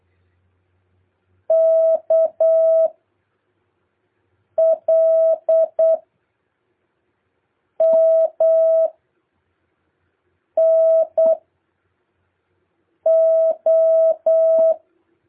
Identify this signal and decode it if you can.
Morse code, heard as a keyed tone, about 8 wpm, 650 Hz. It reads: K L M N O